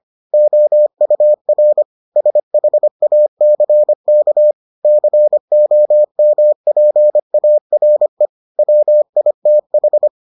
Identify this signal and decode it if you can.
Morse code, heard as a keyed tone, about 25 wpm, 600 Hz. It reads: OUR SHACK COMPARE WITH